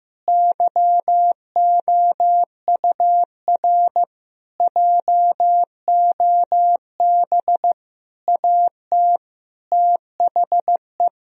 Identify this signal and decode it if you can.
Morse code, heard as a keyed tone, about 15 wpm, 700 Hz. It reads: YOUR JOB AT THE
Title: Sentences from Top 1000 Words - Review Part 4 - 15wpm